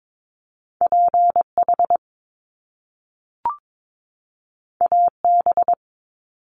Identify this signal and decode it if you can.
Morse code, heard as a keyed tone, about 22 wpm, 700 Hz. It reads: PH AB
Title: Top Two Letter Combinations in Words 22wpm